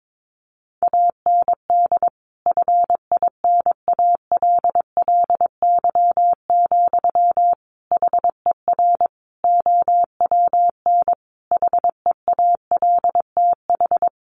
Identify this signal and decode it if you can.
Morse code, heard as a keyed tone, about 22 wpm, 700 Hz. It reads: AND FINALLY, HER OWN HEALTH
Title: Sentences from Top 1100 Words 22wpm